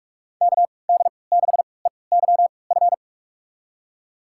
Morse code, heard as a keyed tone, about 46 wpm, spaced as for 26 wpm, 700 Hz. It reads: KD6EYF
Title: USA Call Signs - 3x Character Spacing - 45wpm